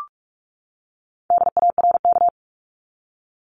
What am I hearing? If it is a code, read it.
Morse code, heard as a keyed tone, about 44 wpm, 700 Hz. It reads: DARK
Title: Top 500 Words 45wpm